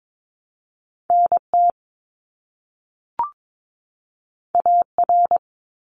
Morse code, heard as a keyed tone, about 22 wpm, 700 Hz. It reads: NT AR
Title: ARRL Sections 22wpm